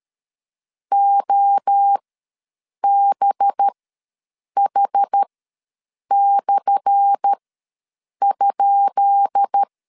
Morse code, heard as a keyed tone, about 13 wpm, spaced as for 7 wpm, 800 Hz. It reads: OBH/?
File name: LEZIONI / 10 SIGNS / 10B INT ws 12wpm 3527